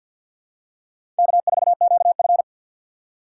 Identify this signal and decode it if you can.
Morse code, heard as a keyed tone, about 49 wpm, 700 Hz. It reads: K4QF
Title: USA Call Signs - 4 Characters - 2x Character Spacing - 50wpm